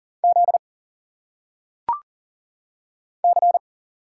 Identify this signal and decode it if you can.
Morse code, heard as a keyed tone, about 39 wpm, 700 Hz. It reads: Z C
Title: Single Letters - Phonetic Pronunciation 40wpm